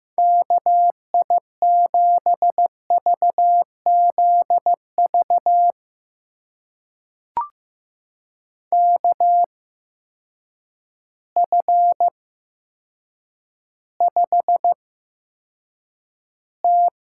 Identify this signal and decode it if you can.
Morse code, heard as a keyed tone, about 15 wpm, spaced as for 8 wpm, 700 Hz. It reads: KI7VZV KF5T
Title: USA Call Signs - 6 Characters - 8x Character Spacing - 15wpm